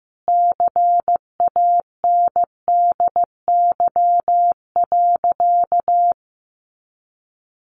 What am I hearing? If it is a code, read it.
Morse code, heard as a keyed tone, about 15 wpm, 700 Hz. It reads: CANDY.
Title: Easy Phrases 15wpm